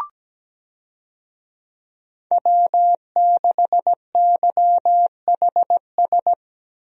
Morse code, heard as a keyed tone, about 17 wpm, 700 Hz. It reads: W6YHS
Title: USA Call Signs - 5 Characters - 17wpm